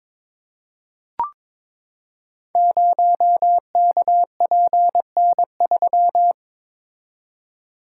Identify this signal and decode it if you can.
Morse code, heard as a keyed tone, about 22 wpm, 700 Hz. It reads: 0KPN3
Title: Code Groups - 5 Letter:Numbers - 22wpm